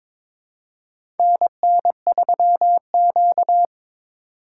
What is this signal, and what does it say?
Morse code, heard as a keyed tone, about 22 wpm, 700 Hz. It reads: NN3Q
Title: USA Call Signs - 4x Character Spacing - 22wpm